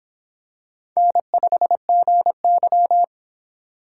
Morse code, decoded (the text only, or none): N5GY